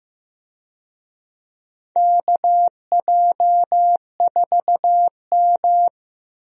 Morse code, heard as a keyed tone, about 15 wpm, 700 Hz. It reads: KJ4M
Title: USA Call Signs - 4 Characters - 15wpm